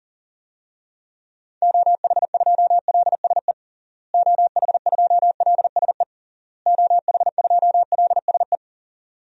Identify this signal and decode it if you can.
Morse code, heard as a keyed tone, about 40 wpm, 700 Hz. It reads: OH2LSE OH2LSE OH2LSE
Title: DXCC-100 Call Signs - 6 Characters - Repeated 3x - 40wpm